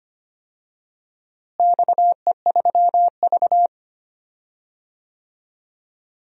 Morse code, heard as a keyed tone, about 25 wpm, 700 Hz. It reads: XE3V
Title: DXCC-100 Call Signs - 4 Characters - Repeated 2x - 25wpm